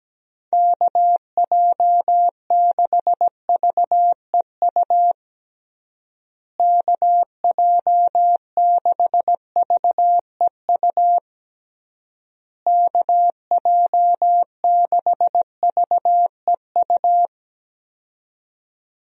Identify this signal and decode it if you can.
Morse code, heard as a keyed tone, about 17 wpm, 700 Hz. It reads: KJ6VEU KJ6VEU KJ6VEU
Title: USA Call Signs - 6 Characters - Repeated 3x - 17wpm